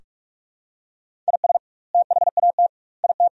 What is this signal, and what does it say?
Morse code, heard as a keyed tone, about 46 wpm, 700 Hz. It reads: IS THAT IT
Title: 2x Word-Spacing - Sentences from Top 100 Words - 45wpm